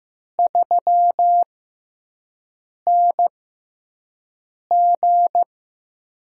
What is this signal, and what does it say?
Morse code, heard as a keyed tone, about 15 wpm, spaced as for 5 wpm, 700 Hz. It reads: 3NG
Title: USA Call Signs - 4 Characters - 6x Character Spacing - 15wpm